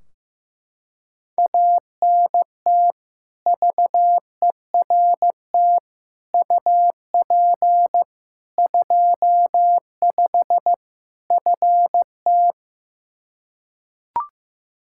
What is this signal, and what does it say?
Morse code, heard as a keyed tone, about 15 wpm, 700 Hz. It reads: ANT VERT UP 25 FT
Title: Practice QSO - Exchange 2 - Info - 15wpm